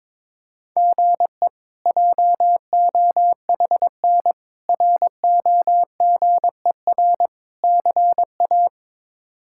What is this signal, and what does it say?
Morse code, heard as a keyed tone, about 22 wpm, 700 Hz. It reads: GE JOHN ROGER CA
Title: K1USN SST QSO Practice - 22wpm